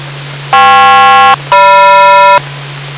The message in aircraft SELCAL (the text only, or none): KP-GM